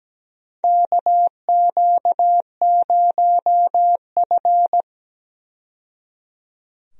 Morse code, heard as a keyed tone, about 17 wpm, 700 Hz. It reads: KQ0F
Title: USA Call Signs - 4 Characters - 17wpm